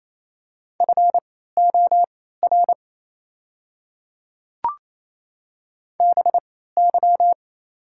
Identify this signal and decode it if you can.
Morse code, heard as a keyed tone, about 28 wpm, spaced as for 16 wpm, 700 Hz. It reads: FOR BY
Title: 3x Character Spacing - Top 100 Words - 28wpm